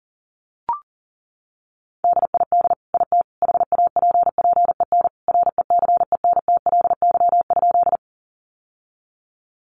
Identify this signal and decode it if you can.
Morse code, heard as a keyed tone, about 40 wpm, 700 Hz. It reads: DID IT HAPPEN RECENTLY?
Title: Sentences from Top 1200 Words 40wpm